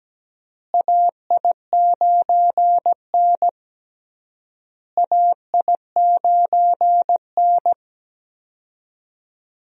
Morse code, heard as a keyed tone, about 17 wpm, 700 Hz. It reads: AI9N AI9N
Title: USA Call Signs - 4 Characters - Repeated 3x - 17wpm